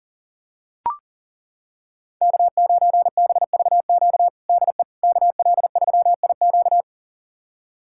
Morse code, heard as a keyed tone, about 40 wpm, 700 Hz. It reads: K9BVQ DE KL3IQ
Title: USA Callsign DE Callsign - 5 Characters - 40wpm